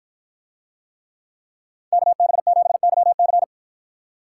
Morse code, heard as a keyed tone, about 53 wpm, 700 Hz. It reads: KB7YC